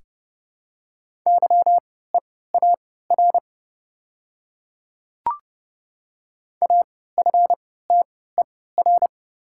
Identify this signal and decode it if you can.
Morse code, heard as a keyed tone, about 30 wpm, spaced as for 17 wpm, 690 Hz. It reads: YEAR AFTER
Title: 3x Character Spacing - Top 200 Words - 30wpm